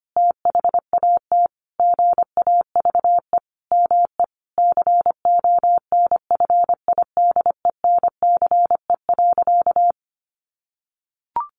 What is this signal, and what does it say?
Morse code, heard as a keyed tone, about 25 wpm, 700 Hz. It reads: THAT GAVE ME CONFIDENCE.
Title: Sentences from Top 1500 Words 25wpm